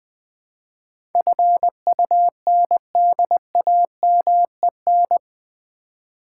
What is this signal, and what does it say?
Morse code, heard as a keyed tone, about 20 wpm, 700 Hz. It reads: FUNDAMEN
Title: Build A Word - 4 Letters At A Time - 11-Letter Words - 20wpm